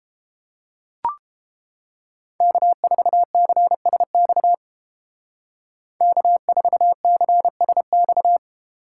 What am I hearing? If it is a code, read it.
Morse code, heard as a keyed tone, about 29 wpm, 700 Hz. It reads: K4CSX K4CSX